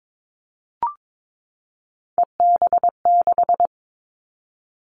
Morse code, heard as a keyed tone, about 22 wpm, 700 Hz. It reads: EB6
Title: Callsign Prefixes for Top 50 DXCC - 22wpm